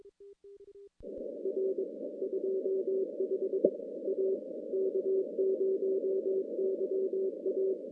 Morse code, heard as a keyed tone, about 23 wpm, 380 Hz, with noise. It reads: R 25 A K0YA
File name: contests / CW-SS-k5tr- -2200